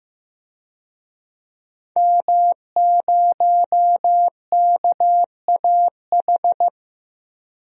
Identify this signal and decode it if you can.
Morse code, heard as a keyed tone, about 15 wpm, 700 Hz. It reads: M0KAH
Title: DXCC-100 Call Signs - 5 Characters - Repeated 3x - 15wpm